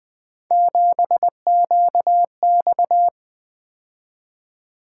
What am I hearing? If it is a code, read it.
Morse code, heard as a keyed tone, about 20 wpm, 700 Hz. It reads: 7QX
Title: Three Letter-Number Combinations - 20wpm